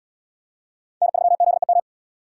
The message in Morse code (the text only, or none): N3ZEM